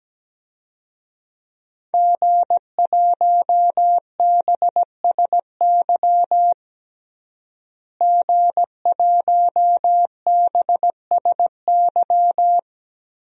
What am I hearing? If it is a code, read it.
Morse code, heard as a keyed tone, about 17 wpm, 700 Hz. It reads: G1BSY G1BSY
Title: DXCC-100 Call Signs - 5 Characters - Repeated 3x - 17wpm